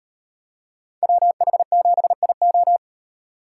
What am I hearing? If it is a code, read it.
Morse code, heard as a keyed tone, about 37 wpm, 700 Hz. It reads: WH7IO